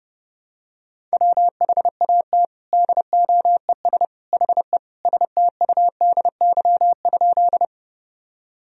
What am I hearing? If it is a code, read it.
Morse code, heard as a keyed tone, about 30 wpm, 700 Hz. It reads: WHAT DOES HE STUDY?
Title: Sentences from Top 1000 Words - Review Part 1 - 30wpm